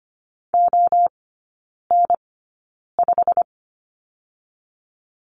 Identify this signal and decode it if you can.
Morse code, heard as a keyed tone, about 25 wpm, spaced as for 9 wpm, 700 Hz. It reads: ON5
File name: Lesson 15e - Number 5 - Three Characters 25wpm